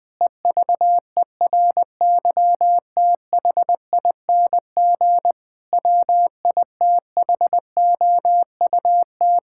Morse code, heard as a keyed tone, about 20 wpm, 700 Hz. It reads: EVERYTHING WITHOUT
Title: Sentences from Top 1000 Words - Review Part 6 - 20wpm